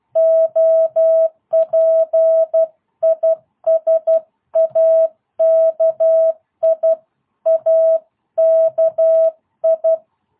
Morse code, heard as a keyed tone, about 12 wpm, 660 Hz. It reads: OP IS AKI AKI